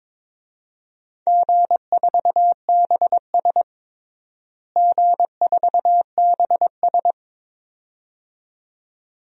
Morse code, heard as a keyed tone, about 21 wpm, 700 Hz. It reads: G4BS G4BS
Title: DXCC-100 Call Signs - 4 Characters - Repeated 2x - 22wpm